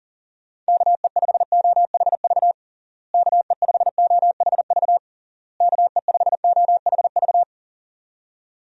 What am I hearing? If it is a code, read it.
Morse code, heard as a keyed tone, about 40 wpm, 700 Hz. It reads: KE5OHV KE5OHV KE5OHV
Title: USA Call Signs - 6 Characters - Repeated 3x - 40wpm